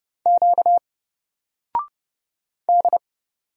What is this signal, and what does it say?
Morse code, heard as a keyed tone, about 30 wpm, 700 Hz. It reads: Q D
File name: Single-Letter-Number-30wpm